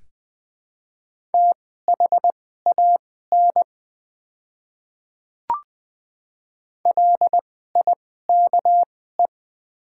Morse code, heard as a keyed tone, about 20 wpm, spaced as for 14 wpm, 700 Hz. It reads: THAN LIKE
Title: 2x Character Spacing - Top 100 Words - 20wpm